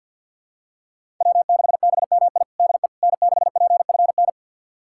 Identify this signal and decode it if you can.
Morse code, heard as a keyed tone, about 52 wpm, 700 Hz. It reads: W6BMI DE N6PFN
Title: USA Callsign DE Callsign - 5 Characters - 50wpm